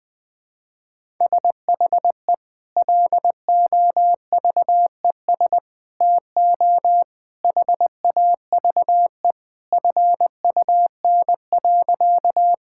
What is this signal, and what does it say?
Morse code, heard as a keyed tone, about 21 wpm, 700 Hz. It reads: SHE LOVES TO HAVE FUN.
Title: Sentences with 2-Word Phrases - Spoken Beginning - 20wpm